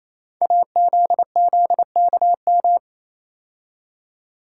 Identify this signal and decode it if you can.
Morse code, heard as a keyed tone, about 29 wpm, 700 Hz. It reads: AZZKM
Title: Code Groups - 5 Letters - 28wpm